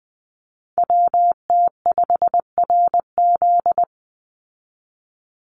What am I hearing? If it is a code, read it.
Morse code, heard as a keyed tone, about 20 wpm, 700 Hz. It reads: WT5RZ
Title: Call Signs Repeated 3x - 20wpm